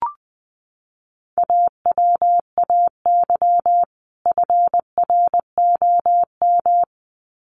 Morse code, heard as a keyed tone, about 20 wpm, 700 Hz. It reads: AWAY FROM
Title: Sets of 2 Words 20wpm